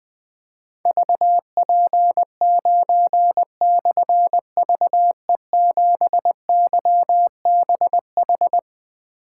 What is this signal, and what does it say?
Morse code, heard as a keyed tone, about 20 wpm, 700 Hz. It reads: VP9/VE7YBH
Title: Difficult Call Signs 20wpm